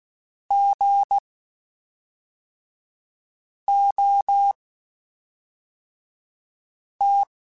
Morse code, heard as a keyed tone, about 16 wpm, spaced as for 3 wpm, 800 Hz. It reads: GOT
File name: Short Runs / Short Runs-mp3s / Gotten @5